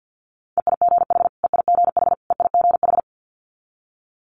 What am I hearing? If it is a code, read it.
Morse code, heard as a keyed tone, about 50 wpm, 700 Hz. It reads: EIGH EIGH EIGH
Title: Multi-letter Phonemes Repeated 3x 50wpm